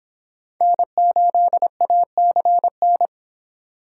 Morse code, heard as a keyed tone, about 26 wpm, 700 Hz. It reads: N8ACN